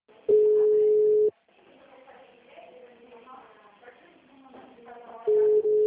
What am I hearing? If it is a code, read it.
Ringback tone (Europe, 425 Hz), with speech.
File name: out- -2009- - - .13930